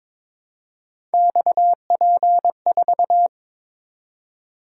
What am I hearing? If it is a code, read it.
Morse code, heard as a keyed tone, about 22 wpm, 700 Hz. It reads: XP4
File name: practice / 22wpm / Three Letter-Number Combinations - 22wpm